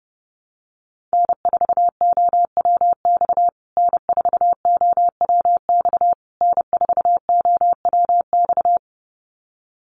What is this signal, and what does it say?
Morse code, heard as a keyed tone, about 30 wpm, 700 Hz. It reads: N4OWX N4OWX N4OWX
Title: USA Call Signs Repeated 3x 30wpm